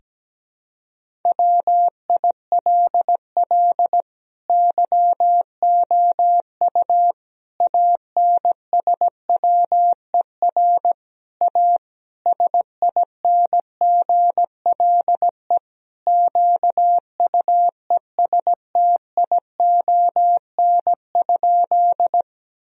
Morse code, heard as a keyed tone, about 17 wpm, 700 Hz. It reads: WILL YOU ANSWER A SINGLE QUESTION?
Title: Sentences from Top 1000 Words - Review Part 4 - 17wpm